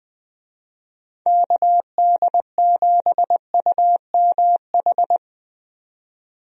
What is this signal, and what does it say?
Morse code, heard as a keyed tone, about 21 wpm, 700 Hz. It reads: KD7UMH
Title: USA Call Signs - 6 Characters - 20wpm